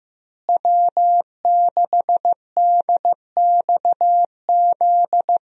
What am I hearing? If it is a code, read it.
Morse code, heard as a keyed tone, about 15 wpm, 700 Hz. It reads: W6DXZ
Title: USA Call Signs - 5 Characters - Repeated 2x - 15wpm